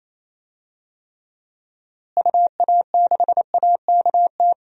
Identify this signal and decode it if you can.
Morse code, heard as a keyed tone, about 28 wpm, 700 Hz. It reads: UA6AKT
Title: DXCC-100 Call Signs - 6 Characters - Repeated 3x - 28wpm